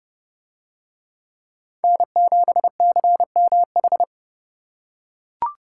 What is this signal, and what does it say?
Morse code, heard as a keyed tone, about 27 wpm, 700 Hz. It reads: N7CMH